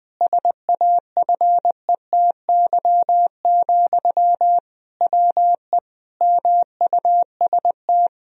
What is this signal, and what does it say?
Morse code, heard as a keyed tone, about 20 wpm, 700 Hz. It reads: SAFETY, WE MUST
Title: Sentences from Top 1000 Words - Review Part 4 - 20wpm